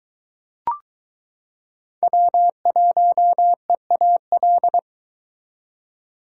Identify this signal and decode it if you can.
Morse code, heard as a keyed tone, about 22 wpm, 700 Hz. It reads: W1EAL